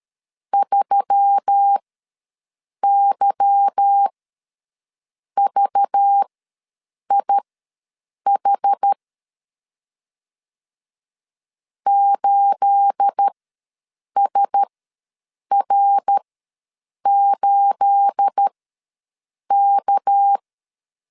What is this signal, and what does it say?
Morse code, heard as a keyed tone, about 13 wpm, spaced as for 7 wpm, 800 Hz. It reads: 3YVIH 8SR8K
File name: LEZIONI / 9 28 ws 12wpm 3354